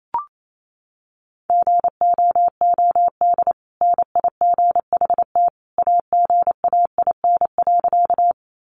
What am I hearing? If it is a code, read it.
Morse code, heard as a keyed tone, about 28 wpm, 700 Hz. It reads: GOOD NIGHT AGAIN.
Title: Sentences from Top 300 Words 28wpm